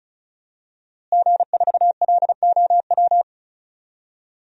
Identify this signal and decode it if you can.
Morse code, heard as a keyed tone, about 36 wpm, 700 Hz. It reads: G4LOW
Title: DXCC-100 Call Signs - 5 Characters - Repeated 2x - 35wpm